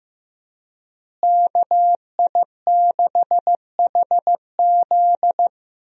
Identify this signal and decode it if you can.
Morse code, heard as a keyed tone, about 15 wpm, 700 Hz. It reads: KI6HZ
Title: USA Call Signs - 5 Characters - Repeated 3x - 15wpm